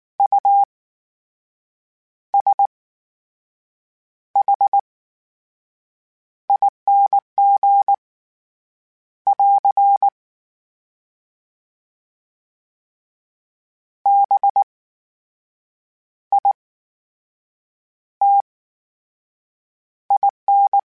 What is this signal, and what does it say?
Morse code, heard as a keyed tone, about 19 wpm, spaced as for 6 wpm, 800 Hz. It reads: USHING <AR> BITIN
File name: Special INGs / Special INGs-mp3s / VerySlowToFaster Ing @18